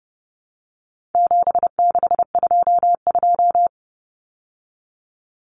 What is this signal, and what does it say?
Morse code, heard as a keyed tone, about 30 wpm, 700 Hz. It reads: 7622
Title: Common Cut Numbers - 4-Digit 30wpm